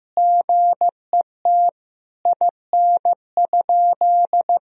Morse code, heard as a keyed tone, about 15 wpm, 700 Hz. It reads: GET IN?
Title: Sentences from Top 1000 Words - Review Part 2 - 15wpm